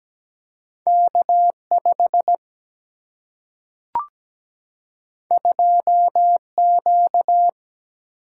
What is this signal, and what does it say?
Morse code, heard as a keyed tone, about 17 wpm, 700 Hz. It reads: K 5 2 Q
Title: Single-Letter Single-Number 17wpm